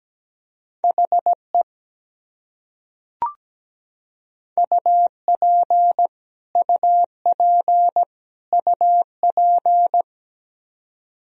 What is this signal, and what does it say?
Morse code, heard as a keyed tone, about 17 wpm, 700 Hz. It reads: HE UP UP UP
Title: Top 100 Words Repeated 17wpm